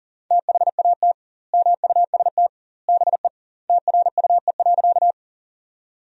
Morse code, heard as a keyed tone, about 40 wpm, 700 Hz. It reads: THAT MUST BE TRUE.
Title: 2x Word-Spacing - Sentences from Top 400 Words - 40wpm